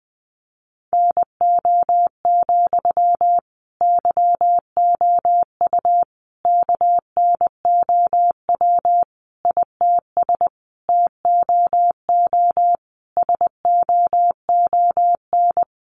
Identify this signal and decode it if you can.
Morse code, heard as a keyed tone, about 20 wpm, 700 Hz. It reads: NO, YOU KNOW ITS TOO SOON
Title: Sentences from Top 800 Words 20wpm